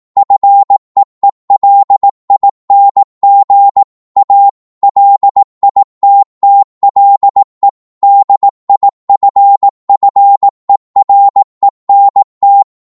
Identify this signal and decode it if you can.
Morse code, heard as a keyed tone, about 18 wpm, 800 Hz. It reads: FEELING A LITTLE DIFFERENT
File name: Morse / psg-18wpm